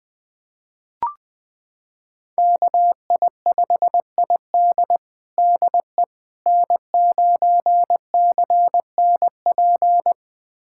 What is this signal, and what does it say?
Morse code, heard as a keyed tone, about 21 wpm, 700 Hz. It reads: KI5ID DE N9CNP
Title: USA Callsign DE Callsign - 5 Characters - 20wpm